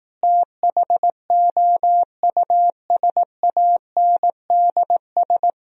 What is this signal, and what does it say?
Morse code, heard as a keyed tone, about 18 wpm, 700 Hz. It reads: THOUSANDS
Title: Speed-Racing - Sets of 4 Words 15wpm